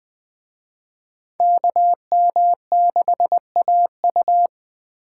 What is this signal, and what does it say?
Morse code, heard as a keyed tone, about 20 wpm, 700 Hz. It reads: KM6AU